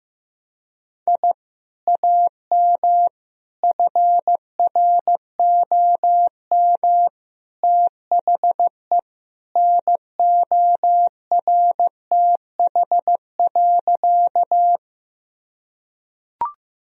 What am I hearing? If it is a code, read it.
Morse code, heard as a keyed tone, about 14 wpm, 700 Hz. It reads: I AM FROM THE NORTH.
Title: Sentences from Top 1000 Words - Review Part 1 - 15wpm